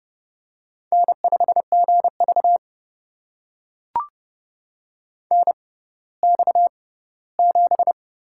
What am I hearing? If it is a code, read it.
Morse code, heard as a keyed tone, about 30 wpm, spaced as for 17 wpm, 700 Hz. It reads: N5GV NX7
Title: USA Call Signs - 6x Character Spacing - 30wpm